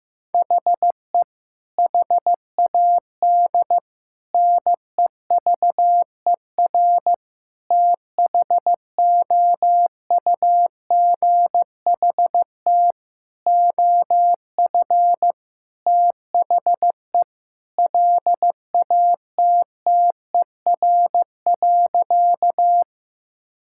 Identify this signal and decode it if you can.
Morse code, heard as a keyed tone, about 15 wpm, 700 Hz. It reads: HE HAD NEVER THOUGHT OF THE LATTER.
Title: Sentences from Top 1600 Words 15wpm